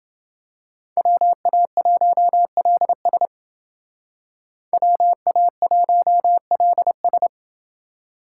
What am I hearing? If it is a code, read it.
Morse code, heard as a keyed tone, about 28 wpm, 700 Hz. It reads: WA1LS WA1LS